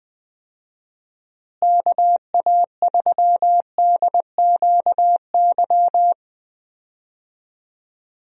Morse code, heard as a keyed tone, about 20 wpm, 700 Hz. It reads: KA3DQY
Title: USA Call Signs - 6 Characters - Repeated 3x - 20wpm